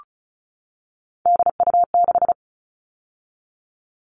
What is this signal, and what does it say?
Morse code, heard as a keyed tone, about 34 wpm, 700 Hz. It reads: DU6
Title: All Cut Numbers - 3-Digit 35wpm